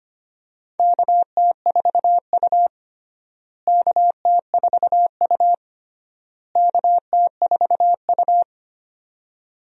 Morse code, heard as a keyed tone, about 25 wpm, 700 Hz. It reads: KT4U KT4U KT4U
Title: USA Call Signs - 4 Characters - Repeated 3x - 25wpm